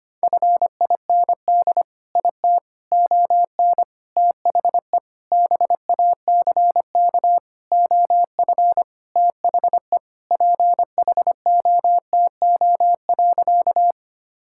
Morse code, heard as a keyed tone, about 25 wpm, 700 Hz. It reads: FIND IT ON THE BACK OF THE PHOTO.
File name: practice / 25wpm / Sentences with 4-Word Phrases - 25wpm